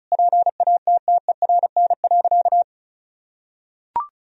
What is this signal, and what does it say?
Morse code, heard as a keyed tone, about 36 wpm, 700 Hz. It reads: PATTERN.
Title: Sentences from Top 1000 Words - Review Part 2 - 35wpm